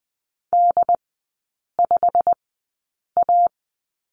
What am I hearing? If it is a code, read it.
Morse code, heard as a keyed tone, about 20 wpm, spaced as for 8 wpm, 700 Hz. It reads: D5A
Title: 172 Lesson 28e - Number 9 - Three Characters 20wpm